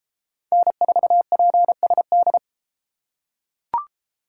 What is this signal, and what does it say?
Morse code, heard as a keyed tone, about 33 wpm, 700 Hz. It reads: N4PSD